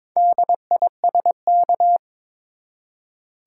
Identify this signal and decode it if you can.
Morse code, heard as a keyed tone, about 21 wpm, 700 Hz. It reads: DISK
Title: Four Letter Words - 22wpm